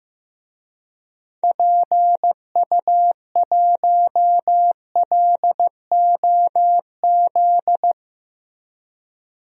Morse code, heard as a keyed tone, about 15 wpm, 700 Hz. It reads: PU1LOZ
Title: DXCC-100 Call Signs - 6 Characters - 15wpm